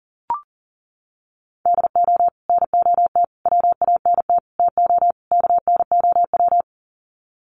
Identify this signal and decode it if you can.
Morse code, heard as a keyed tone, about 40 wpm, 700 Hz. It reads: DO NOT WANT TO KNOW
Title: Sets of 5 Words Encore 40wpm